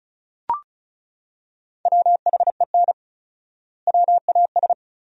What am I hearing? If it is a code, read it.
Morse code, heard as a keyed tone, about 34 wpm, 700 Hz. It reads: WHEN WAS
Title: 4x Word-Spacing - Sentences from Top 100 Words - 35wpm